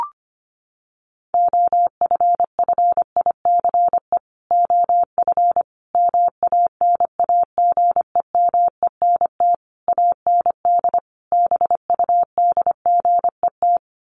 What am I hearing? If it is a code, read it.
Morse code, heard as a keyed tone, about 25 wpm, 700 Hz. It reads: OFFICE OF MANAGEMENT AND BUDGET
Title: Sets of 5 Words Encore2 25wpm